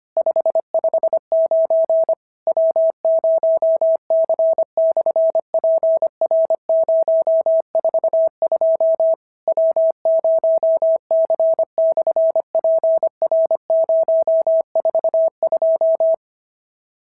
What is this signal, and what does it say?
Morse code, heard as a keyed tone, about 25 wpm, 650 Hz. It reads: 559 W0C/PR042 W0C/PR042
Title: SOTA QSO Practice 25wpm